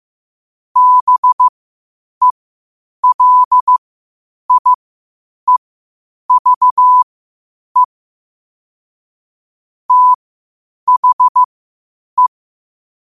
Morse code, heard as a keyed tone, about 15 wpm, spaced as for 8 wpm, 1000 Hz. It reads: BELIEVE THE
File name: qotdfiles / 8wpm